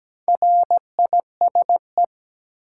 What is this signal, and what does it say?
Morse code, heard as a keyed tone, about 17 wpm, 700 Hz. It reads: RISE
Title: Sentences from Top 900 Words - New Word Drill - 17wpm